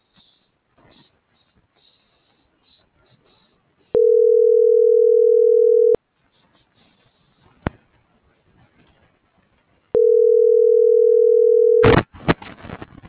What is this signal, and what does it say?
Ringback tone (North America), 440 + 480 Hz.